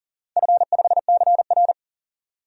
Morse code, heard as a keyed tone, about 40 wpm, 700 Hz. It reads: F5CR
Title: DXCC-100 Call Signs - 4 Characters - 40wpm